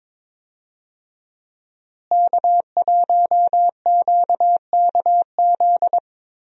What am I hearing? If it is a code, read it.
Morse code, heard as a keyed tone, about 22 wpm, 700 Hz. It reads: K1QKZ
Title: USA Call Signs - 5 Characters - Repeated 2x - 22wpm